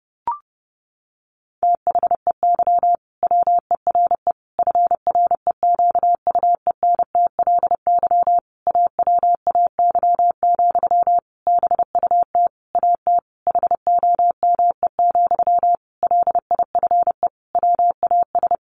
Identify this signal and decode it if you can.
Morse code, heard as a keyed tone, about 30 wpm, 700 Hz. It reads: THEY WERE FREQUENTLY AWAY, BUT AT HOME, LIFE WAS
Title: Sentences from Top 1400 Words 30wpm